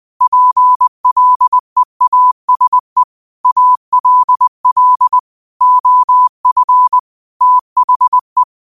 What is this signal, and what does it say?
Morse code, heard as a keyed tone, about 20 wpm, 1000 Hz. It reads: PLEASE ALL OF THE
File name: 20wpm